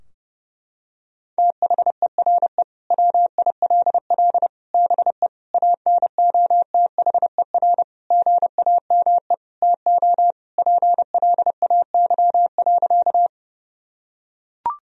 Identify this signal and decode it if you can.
Morse code, heard as a keyed tone, about 30 wpm, 700 Hz. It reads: THERE WILL BE ANOTHER GAME TO PLAY.
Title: Sentences from Top 1000 Words - Review Part 3 - 30wpm